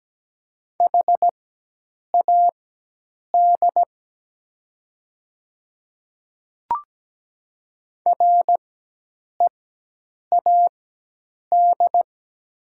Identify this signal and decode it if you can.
Morse code, heard as a keyed tone, about 17 wpm, spaced as for 8 wpm, 700 Hz. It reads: HAD READ
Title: 4x Character Spacing - Top 200 Words - 17wpm